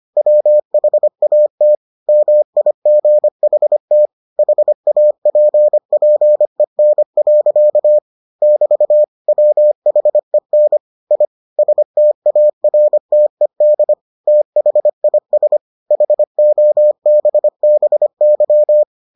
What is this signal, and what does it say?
Morse code, heard as a keyed tone, about 25 wpm, 600 Hz. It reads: WHAT MIGHT HAPPEN. <BT> WHEN I STARTED THIS HOBBY